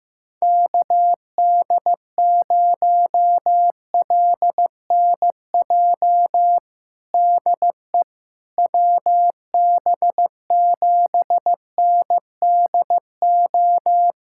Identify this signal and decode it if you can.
Morse code, heard as a keyed tone, about 15 wpm, 700 Hz. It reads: KD0LNJ DE WB7NDO
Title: USA Callsign DE Callsign 15wpm